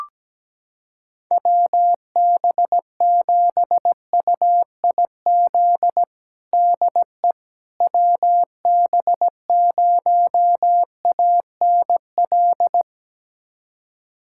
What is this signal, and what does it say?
Morse code, heard as a keyed tone, about 17 wpm, 700 Hz. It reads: WB7UIZ DE WB0ANL
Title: USA Callsign DE Callsign - 6 Characters - 17wpm